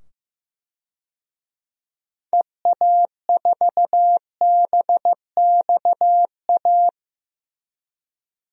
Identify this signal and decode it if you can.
Morse code, heard as a keyed tone, about 15 wpm, 700 Hz. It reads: EA4BXA
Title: DXCC-100 Call Signs - 6 Characters - 15wpm